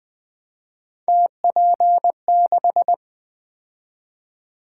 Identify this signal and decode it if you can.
Morse code, heard as a keyed tone, about 20 wpm, 700 Hz. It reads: TP6
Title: Three Letter-Number Combinations - 20wpm